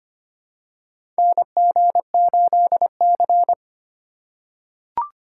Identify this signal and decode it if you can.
Morse code, heard as a keyed tone, about 25 wpm, 700 Hz. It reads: NG8C
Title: USA Call Signs - 4 Characters - 4x Character Spacing - 25wpm